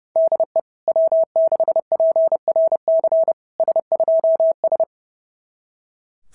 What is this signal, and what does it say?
Morse code, heard as a keyed tone, about 30 wpm, 690 Hz. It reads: DE W6PRC S2S
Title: SOTA QSO Practice 30wpm